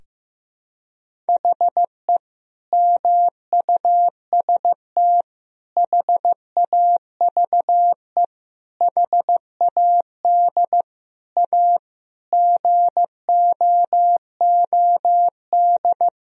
Morse code, heard as a keyed tone, about 15 wpm, 700 Hz. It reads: HE MUST HAVE HAD A GOOD
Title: Sentences from Top 1000 Words - Review Part 1 - 15wpm